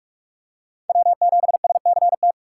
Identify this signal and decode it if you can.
Morse code, heard as a keyed tone, about 43 wpm, 700 Hz. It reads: W7SCT
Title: USA Call Signs - 5 Characters - 2x Character Spacing - 45wpm